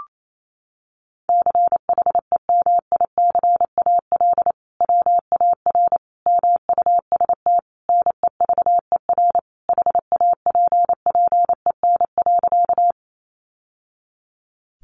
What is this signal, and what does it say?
Morse code, heard as a keyed tone, about 28 wpm, 700 Hz. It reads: CHEMICAL WAR MUST NEVER HAPPEN.
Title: Sentences from Top 1200 Words 28wpm